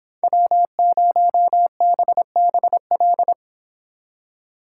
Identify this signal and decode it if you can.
Morse code, heard as a keyed tone, about 25 wpm, 700 Hz. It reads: W0BBL